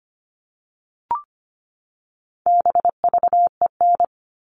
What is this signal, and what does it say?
Morse code, heard as a keyed tone, about 25 wpm, 700 Hz. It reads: BVEN
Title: All Cut Numbers - 4-Digit 25wpm